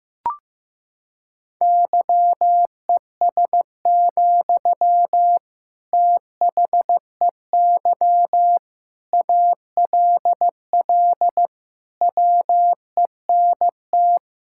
Morse code, heard as a keyed tone, about 15 wpm, 700 Hz. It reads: YES, THEY ALL WENT
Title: Sentences from Top 1600 Words 15wpm